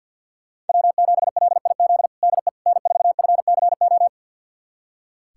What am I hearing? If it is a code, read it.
Morse code, heard as a keyed tone, about 47 wpm, 700 Hz. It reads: W7LIZ DE N4FCO
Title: USA Callsign DE Callsign - 5 Characters - 50wpm